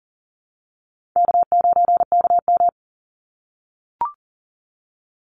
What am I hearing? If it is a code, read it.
Morse code, heard as a keyed tone, about 40 wpm, 700 Hz. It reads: K9KM
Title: Call Signs - 40wpm